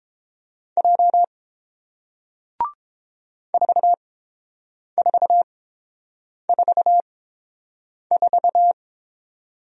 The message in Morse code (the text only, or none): J 4444